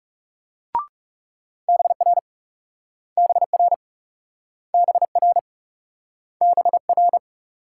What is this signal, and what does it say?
Morse code, heard as a keyed tone, about 36 wpm, 700 Hz. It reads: BR BR BR BR
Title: Speed-Racing Multi-letter Phonemes - 30wpm